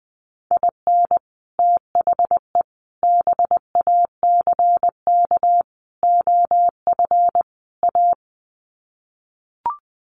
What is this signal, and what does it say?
Morse code, heard as a keyed tone, about 20 wpm, 700 Hz. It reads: IN THE BACK OF A
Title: Sets of 5 Words Encore2 20wpm